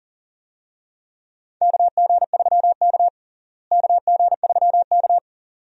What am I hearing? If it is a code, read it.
Morse code, heard as a keyed tone, about 40 wpm, 700 Hz. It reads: KG3K KG3K
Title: USA Call Signs - 4 Characters - Repeated 2x - 40wpm